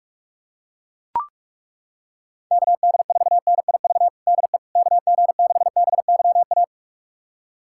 Morse code, heard as a keyed tone, about 42 wpm, 700 Hz. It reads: KD4NIV DE KG6BYA
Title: USA Callsign DE Callsign 45wpm